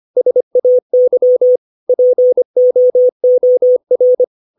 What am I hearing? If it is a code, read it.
Morse code, heard as a keyed tone, about 25 wpm, 500 Hz. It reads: SAY POOR
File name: 632 words / 632 Words 25WPM